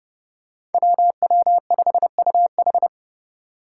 Morse code, heard as a keyed tone, about 30 wpm, 700 Hz. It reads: WW5UH
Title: Five Challenging Letter-Number Combinations 30wpm